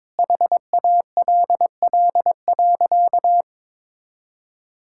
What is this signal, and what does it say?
Morse code, heard as a keyed tone, about 22 wpm, 700 Hz. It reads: HALL.
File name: practice / 22wpm / Sentences with 8-Word Phrases - 22wpm